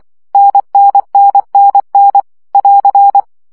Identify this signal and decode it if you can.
Morse code, heard as a keyed tone, about 24 wpm, 800 Hz. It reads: NNNNN <AR>
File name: CW gyakorlas n 20 baud